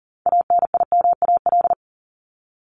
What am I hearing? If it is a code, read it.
Morse code, heard as a keyed tone, about 40 wpm, 700 Hz. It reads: ANIMAL
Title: ES:S Suffix - Top 100 Words with Base Words 40wpm